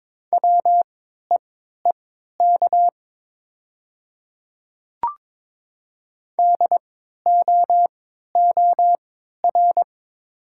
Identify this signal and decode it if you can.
Morse code, heard as a keyed tone, about 22 wpm, spaced as for 12 wpm, 700 Hz. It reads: WEEK DOOR
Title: 3x Character Spacing - Top 500 Words - 22wpm